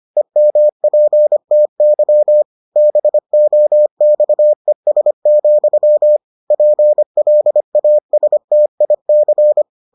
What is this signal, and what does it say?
Morse code, heard as a keyed tone, about 25 wpm, 600 Hz. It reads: EMPTY BOXES, PLASTIC